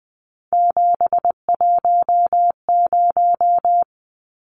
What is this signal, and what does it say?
Morse code, heard as a keyed tone, about 20 wpm, 700 Hz. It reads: 710
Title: Numbers - 3-Digit 20wpm